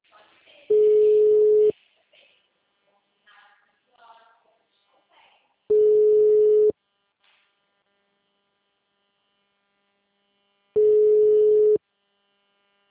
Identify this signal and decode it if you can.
Ringback tone (Europe, 425 Hz), with speech.
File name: out- -2017- - - .7962